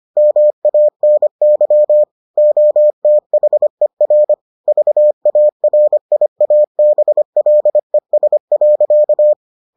Morse code, heard as a keyed tone, about 25 wpm, 600 Hz. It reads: MANY OTHER VARIABLES.